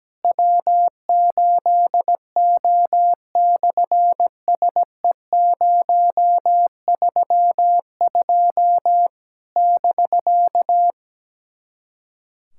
Morse code, heard as a keyed tone, about 17 wpm, 700 Hz. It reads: W8O/SE032 <BK>
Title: SOTA QSO Practice 17wpm